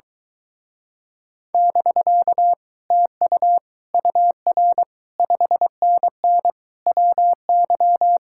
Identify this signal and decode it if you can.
Morse code, heard as a keyed tone, about 24 wpm, 700 Hz. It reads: <BK> TU UR 5NN WY